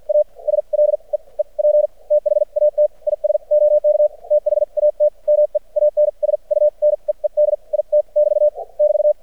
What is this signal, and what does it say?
Morse code, heard as a keyed tone, about 52 wpm, spaced as for 38 wpm, 600 Hz. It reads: ASLEEP THAT IS OK THAT MEANS U NEED IT <BT> <BT>
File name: archived / CRSnet